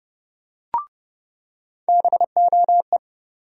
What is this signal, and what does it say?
Morse code, heard as a keyed tone, about 30 wpm, 700 Hz. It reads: BOE
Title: USA Call Sign Postfixes - 3 Letters 30wpm